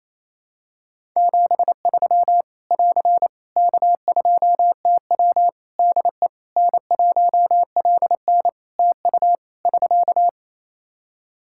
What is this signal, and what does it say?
Morse code, heard as a keyed tone, about 28 wpm, 700 Hz. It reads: 73 <AR> K2TW DE N1LN TU <SK>
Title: Practice QSO - Exchange 4 - Ending - 28wpm